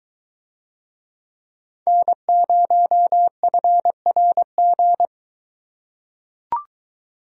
Morse code, heard as a keyed tone, about 21 wpm, 700 Hz. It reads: N0FRG